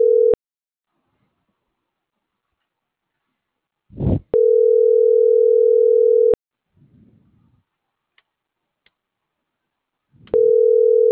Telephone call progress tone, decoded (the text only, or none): ringback tone